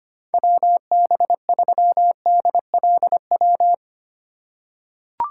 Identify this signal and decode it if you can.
Morse code, heard as a keyed tone, about 25 wpm, 700 Hz. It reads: WB3DLW
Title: USA Call Signs - 6 Characters - 6x Character Spacing - 25wpm